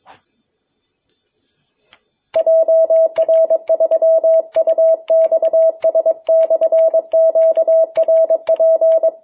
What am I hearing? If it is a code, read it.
Morse code, heard as a keyed tone, about 22 wpm, 660 Hz. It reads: JR3UXS/QRP
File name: jh8cbh / 2021cw